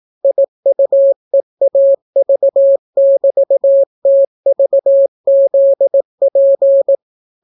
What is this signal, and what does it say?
Morse code, heard as a keyed tone, about 18 wpm, 550 Hz. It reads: IUEAV<BT>TVZP